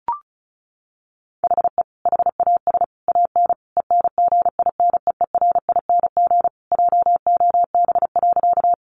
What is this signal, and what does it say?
Morse code, heard as a keyed tone, about 34 wpm, 700 Hz. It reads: HE HAS AN ENGINEERING JOB.
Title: Sentences from Top 1500 Words 35wpm